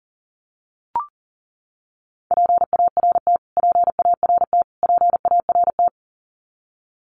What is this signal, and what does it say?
Morse code, heard as a keyed tone, about 40 wpm, 700 Hz. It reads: PART PART PART
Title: Top 200 Words Repeated 40wpm